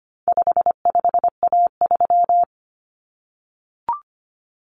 Morse code, heard as a keyed tone, about 25 wpm, 700 Hz. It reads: 55A3
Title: Common Cut Numbers - 4-Digit 25wpm